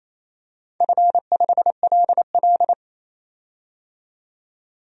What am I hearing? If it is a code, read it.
Morse code, heard as a keyed tone, about 29 wpm, 690 Hz. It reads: F5LL
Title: DXCC-100 Call Signs - 4 Characters - 28wpm